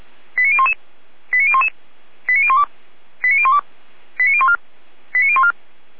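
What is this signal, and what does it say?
Five-tone selective call (ZVEI), heard as quick runs of five tones, tones of about 70 ms. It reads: 89011 89011 89012 89012 89014 89014
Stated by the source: ZVEI five-tone call 89010 received by SDR, from the Feuerwehr Orlamünde audio server.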